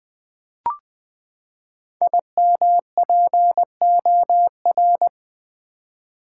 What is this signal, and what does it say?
Morse code, heard as a keyed tone, about 20 wpm, 700 Hz. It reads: IMPOR
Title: Build A Word - 5 Letters At A Time - 10-Letter Words - 20wpm